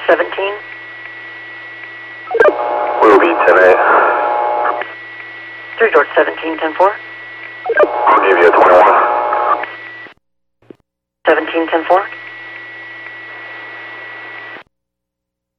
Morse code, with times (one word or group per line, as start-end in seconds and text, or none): none